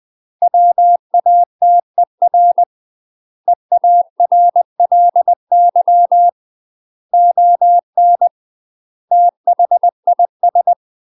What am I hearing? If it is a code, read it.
Morse code, heard as a keyed tone, about 20 wpm, 700 Hz. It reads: WATER EARLY ON THIS
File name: UK MASTER QSOs TRAINING / 20WPM / UK MASTER QSOs 0001